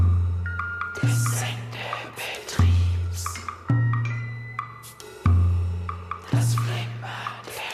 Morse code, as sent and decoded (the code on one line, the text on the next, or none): none